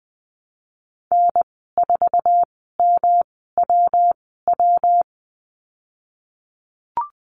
Morse code, heard as a keyed tone, about 20 wpm, spaced as for 15 wpm, 700 Hz. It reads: N4MWW
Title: 154 Lesson 25g - Letter p - Call Signs 20wpm